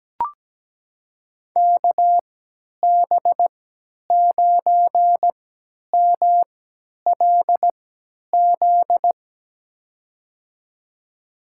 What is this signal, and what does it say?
Morse code, heard as a keyed tone, about 17 wpm, spaced as for 10 wpm, 700 Hz. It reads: KB9MLZ
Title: USA Call Signs - 6 Characters - 3x Character Spacing - 17wpm